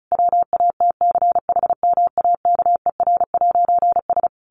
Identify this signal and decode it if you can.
Morse code, heard as a keyed tone, about 34 wpm, 700 Hz. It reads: WATCHMAKER'S
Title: A Sound Of Thunder 35wpm